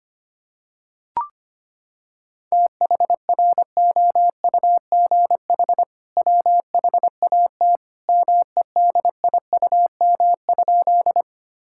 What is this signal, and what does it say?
Morse code, heard as a keyed tone, about 25 wpm, 700 Hz. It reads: THROUGH WHAT MEDIUM?
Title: Sentences from Top 1000 Words - Review Part 6 - 25wpm